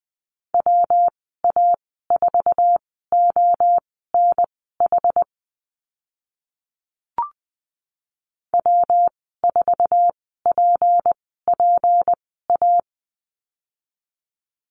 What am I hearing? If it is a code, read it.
Morse code, heard as a keyed tone, about 20 wpm, spaced as for 15 wpm, 700 Hz. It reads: WA4ONH W4PPA
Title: 193 Lesson 31g - Letter v - Call Signs 20wpm